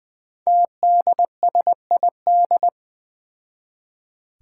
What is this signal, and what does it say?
Morse code, heard as a keyed tone, about 20 wpm, 700 Hz. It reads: TDSID
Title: Five Challenging Letter-Number Combinations 20wpm